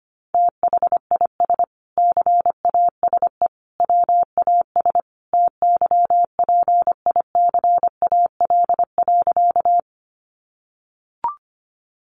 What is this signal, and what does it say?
Morse code, heard as a keyed tone, about 25 wpm, 700 Hz. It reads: THIS CASE WAS TYPICAL.
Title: Sentences from Top 1100 Words 25wpm